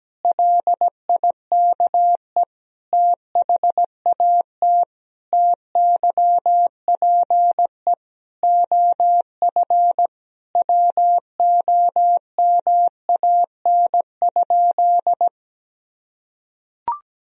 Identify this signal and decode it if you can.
Morse code, heard as a keyed tone, about 17 wpm, 700 Hz. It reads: LIKE THAT TYPE OF WOMAN?
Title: Sentences from Top 1000 Words - Review Part 3 - 17wpm